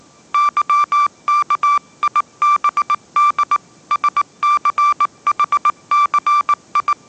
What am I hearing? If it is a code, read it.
Morse code, heard as a keyed tone, about 19 wpm, 1170 Hz. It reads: YKIBD SCHCI
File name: mrx083 20wpm